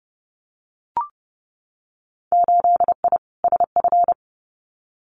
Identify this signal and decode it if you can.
Morse code, heard as a keyed tone, about 30 wpm, 700 Hz. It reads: 8I SF
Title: Winter Field Day Exchanges Encore 30wpm